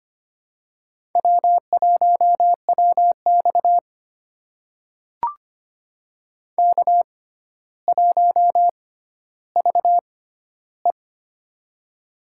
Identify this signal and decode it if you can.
Morse code, heard as a keyed tone, about 25 wpm, spaced as for 13 wpm, 700 Hz. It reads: W1WX K1VE
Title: USA Call Signs - 6x Character Spacing - 25wpm